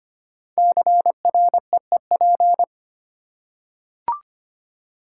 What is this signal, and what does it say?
Morse code, heard as a keyed tone, about 25 wpm, 700 Hz. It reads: CREEP
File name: 5 Letter Words - First 3 Letters Given - 25wpm